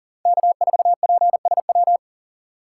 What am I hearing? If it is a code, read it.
Morse code, heard as a keyed tone, about 40 wpm, 700 Hz. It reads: K4PSW
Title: USA Call Signs - 4x Character Spacing - 40wpm